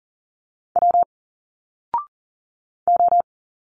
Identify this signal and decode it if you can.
Morse code, heard as a keyed tone, about 40 wpm, 700 Hz. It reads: W O
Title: Single Letters 40wpm